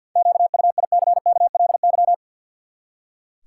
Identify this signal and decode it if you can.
Morse code, heard as a keyed tone, about 46 wpm, 700 Hz. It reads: QUICKLY
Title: Seven Letter Words - 50wpm